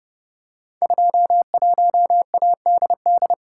Morse code, heard as a keyed tone, about 30 wpm, 700 Hz. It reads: 21ADD
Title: Code Groups - 5 Letter:Numbers - 30wpm